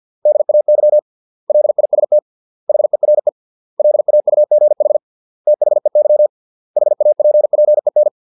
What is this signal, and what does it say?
Morse code, heard as a keyed tone, about 50 wpm, 600 Hz. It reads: DAY LIST HERE LAUGH THEY HAPPEN